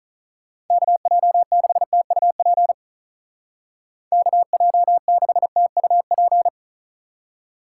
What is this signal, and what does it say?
Morse code, heard as a keyed tone, about 43 wpm, 700 Hz. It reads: KJ6TUP KJ6TUP
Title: Speed Racing - USA 2x3 Call Signs 35wpm